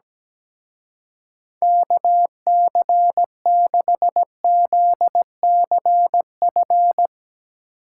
Morse code, heard as a keyed tone, about 17 wpm, 700 Hz. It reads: KC6ZCF
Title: Speed Racing - USA 2x3 Call Signs 17wpm